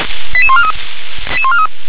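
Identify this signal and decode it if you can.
Five-tone selective call (ZVEI), heard as quick runs of five tones, tones of about 70 ms. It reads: 88143
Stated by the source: Received on SDR as ZVEI five-tone selective call, address 88790.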